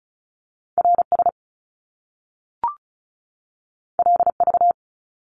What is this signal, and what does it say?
Morse code, heard as a keyed tone, about 36 wpm, 700 Hz. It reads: RS LV
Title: Top Two Letter Combinations in Words 35wpm